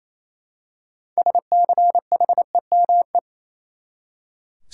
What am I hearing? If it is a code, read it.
Morse code, heard as a keyed tone, about 28 wpm, 700 Hz. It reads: SCHEME
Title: Six Letter Words - 28wpm